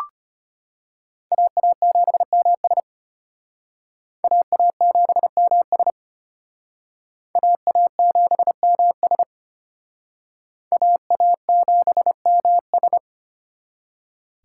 Morse code, heard as a keyed tone, about 32 wpm, 700 Hz. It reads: AA7MS AA7MS AA7MS AA7MS